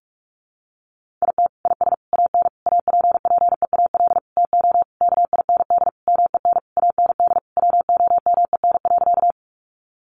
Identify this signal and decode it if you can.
Morse code, heard as a keyed tone, about 44 wpm, 700 Hz. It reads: IT IS AN APPEAL TO KIND MEN AND WOMEN.
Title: Sentences from Top 1300 Words 45wpm